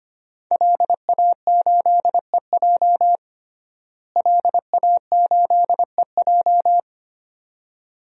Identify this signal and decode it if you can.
Morse code, heard as a keyed tone, about 25 wpm, 700 Hz. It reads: LA8EJ LA8EJ
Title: DXCC-100 Call Signs - 5 Characters - Repeated 2x - 25wpm